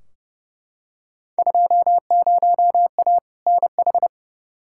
Morse code, heard as a keyed tone, about 30 wpm, 700 Hz. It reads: 20A NH
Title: Field Day Exchanges Encore - 30wpm